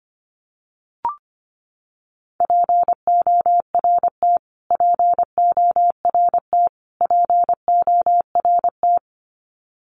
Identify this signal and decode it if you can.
Morse code, heard as a keyed tone, about 25 wpm, 700 Hz. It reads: PORT PORT PORT
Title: Top 200 Words Repeated 25wpm